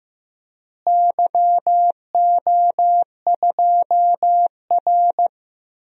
Morse code, heard as a keyed tone, about 15 wpm, 700 Hz. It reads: YO2R
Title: DXCC-100 Call Signs - 4 Characters - Repeated 3x - 15wpm